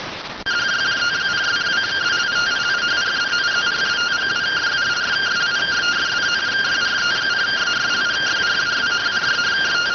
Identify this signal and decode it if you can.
Two-tone teleprinter signal, about 170 Hz shift.